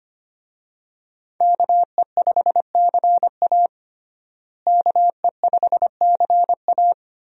Morse code, heard as a keyed tone, about 25 wpm, 690 Hz. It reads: KE5CA KE5CA
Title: USA Call Signs - 5 Characters - Repeated 2x - 25wpm